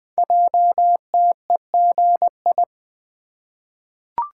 Morse code, heard as a keyed tone, about 20 wpm, 700 Hz. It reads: JTEGI
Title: Code Groups - 5 Letters - 20wpm